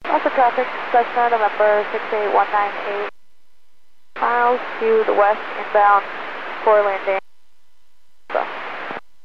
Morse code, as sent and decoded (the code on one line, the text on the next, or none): none